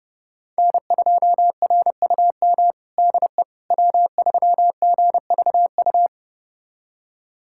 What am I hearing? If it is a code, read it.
Morse code, heard as a keyed tone, about 32 wpm, 700 Hz. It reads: N2RUM DE W3GVU
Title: USA Callsign DE Callsign - 5 Characters - 30wpm